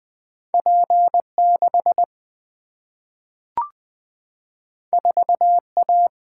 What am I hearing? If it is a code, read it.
Morse code, heard as a keyed tone, about 20 wpm, 700 Hz. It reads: P 6 4 A
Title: Single-Letter Single-Number 20wpm